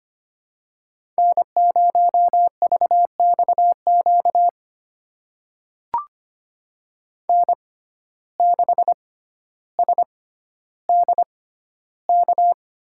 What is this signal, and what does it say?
Morse code, heard as a keyed tone, about 25 wpm, spaced as for 13 wpm, 700 Hz. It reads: N0VXQ N6SDK
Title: USA Call Signs - 6x Character Spacing - 25wpm